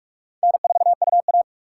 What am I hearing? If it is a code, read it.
Morse code, heard as a keyed tone, about 46 wpm, 700 Hz. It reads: N4UA
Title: USA Call Signs - 4 Characters - 4x Character Spacing - 45wpm